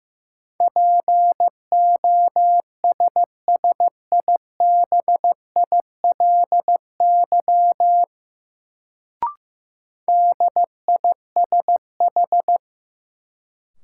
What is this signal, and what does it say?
Morse code, heard as a keyed tone, about 15 wpm, 700 Hz. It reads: POSSIBILY DISH
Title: Sentences from Top 1300 Words - New Word Drill Alt - 15wpm